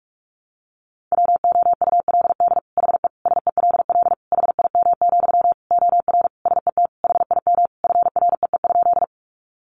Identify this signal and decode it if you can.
Morse code, heard as a keyed tone, about 44 wpm, 700 Hz. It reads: WOULD HE SELL HIM, OR SET HIM FREE?
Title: Sentences from Top 1100 Words 45wpm